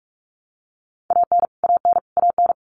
Morse code, heard as a keyed tone, about 44 wpm, 700 Hz. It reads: AN AN AN
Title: Top 500 Words Repeated 45wpm